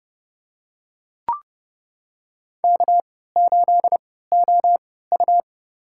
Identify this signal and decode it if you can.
Morse code, heard as a keyed tone, about 29 wpm, spaced as for 17 wpm, 700 Hz. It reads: K8OU
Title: USA Call Signs - 4 Characters - 3x Character Spacing - 30wpm